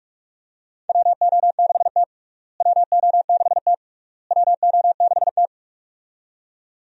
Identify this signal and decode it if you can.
Morse code, heard as a keyed tone, about 46 wpm, 700 Hz. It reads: WO6T WO6T WO6T
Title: USA Call Signs - 4 Characters - Repeated 3x - 45wpm